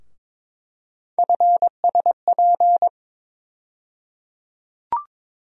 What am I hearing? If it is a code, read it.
Morse code, heard as a keyed tone, about 21 wpm, 700 Hz. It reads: FSP
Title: USA Call Sign Postfixes - 3 Letters 22wpm